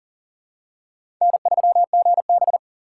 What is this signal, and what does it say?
Morse code, heard as a keyed tone, about 40 wpm, 700 Hz. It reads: N3GB
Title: USA Call Signs - 4 Characters - 4x Character Spacing - 40wpm